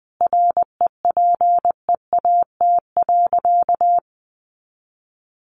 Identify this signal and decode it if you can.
Morse code, heard as a keyed tone, about 20 wpm, 700 Hz. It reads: REPEAT.
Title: A Sound Of Thunder 20wpm